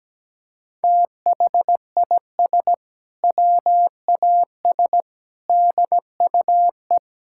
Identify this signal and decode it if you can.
Morse code, heard as a keyed tone, about 17 wpm, 700 Hz. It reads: THIS WAS DUE
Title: Sentences from Top 1600 Words 17wpm